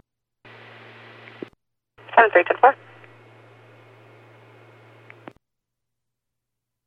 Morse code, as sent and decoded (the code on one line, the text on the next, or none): none